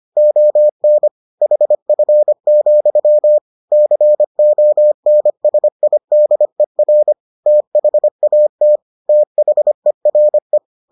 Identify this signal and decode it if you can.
Morse code, heard as a keyed tone, about 25 wpm, 600 Hz. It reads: ON HF, CONSIDER THAT THERE